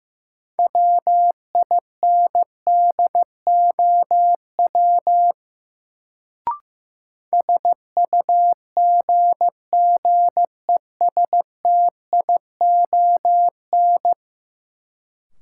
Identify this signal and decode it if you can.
Morse code, heard as a keyed tone, about 15 wpm, 700 Hz. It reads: WINDOW SUGGESTION
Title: Sentences from Top 1600 Words - New Word Drill Alt - 15wpm